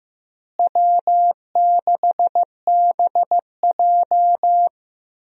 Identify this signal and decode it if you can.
Morse code, heard as a keyed tone, about 15 wpm, 700 Hz. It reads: W6BJ
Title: USA Call Signs - 4 Characters - Repeated 2x - 15wpm